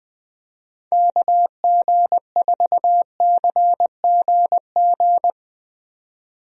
Morse code, heard as a keyed tone, about 20 wpm, 700 Hz. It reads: KG4CGG
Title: USA Call Signs - 6 Characters - 20wpm